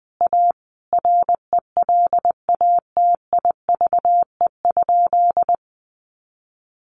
Morse code, heard as a keyed tone, about 20 wpm, 700 Hz. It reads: A RELATIVE?
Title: Sentences from Top 1400 Words 20wpm